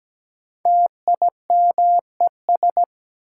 Morse code, heard as a keyed tone, about 17 wpm, 700 Hz. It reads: TIMES
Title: Sentences from Top 1000 Words - Review Part 6 - New Word Drill - 17wpm